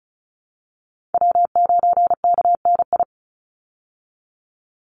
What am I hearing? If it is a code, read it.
Morse code, heard as a keyed tone, about 34 wpm, 700 Hz. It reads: W9KNI
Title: Call Signs Repeated 3x - 35wpm